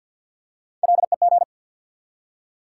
Morse code, heard as a keyed tone, about 48 wpm, 700 Hz. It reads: LEG
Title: Three Letter Words - 50wpm